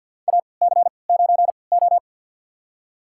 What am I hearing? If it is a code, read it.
Morse code, heard as a keyed tone, about 52 wpm, spaced as for 29 wpm, 700 Hz. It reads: AC9O
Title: USA Call Signs - 4 Characters - 3x Character Spacing - 50wpm